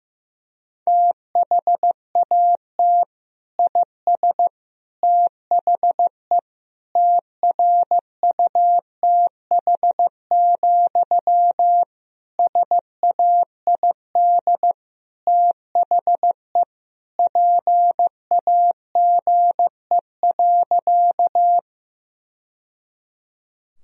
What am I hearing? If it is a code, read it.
Morse code, heard as a keyed tone, about 16 wpm, 700 Hz. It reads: THAT IS THE TRUTH, SAID THE PAGE.
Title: Sentences from Top 1000 Words - Review Part 1 - 15wpm